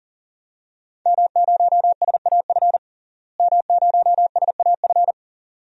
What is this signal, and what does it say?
Morse code, heard as a keyed tone, about 40 wpm, 700 Hz. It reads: M0SAF M0SAF
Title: DXCC-100 Call Signs - 5 Characters - Repeated 2x - 40wpm